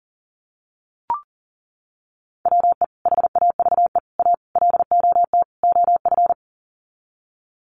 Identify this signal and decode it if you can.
Morse code, heard as a keyed tone, about 40 wpm, 700 Hz. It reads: WE HAVE A LOT OF
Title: Sets of 5 Words 40wpm